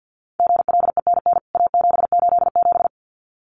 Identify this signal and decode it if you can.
Morse code, heard as a keyed tone, about 50 wpm, 700 Hz. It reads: GLENN A787
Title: CWT Exchange 50wpm